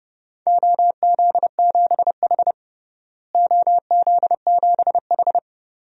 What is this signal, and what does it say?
Morse code, heard as a keyed tone, about 30 wpm, 700 Hz. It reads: OZ7H OZ7H
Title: DXCC-100 Call Signs - 4 Characters - Repeated 2x - 30wpm